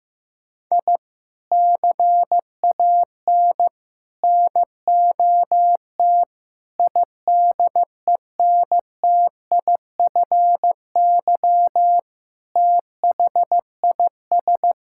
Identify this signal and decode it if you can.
Morse code, heard as a keyed tone, about 15 wpm, 700 Hz. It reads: I CAN NOT IDENTIFY THIS
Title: Sentences from Top 1000 Words - Review Part 6 - 15wpm